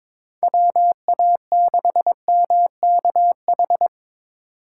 Morse code, heard as a keyed tone, about 22 wpm, 700 Hz. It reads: WA6MKH
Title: USA Call Signs - 6 Characters - 2x Character Spacing - 22wpm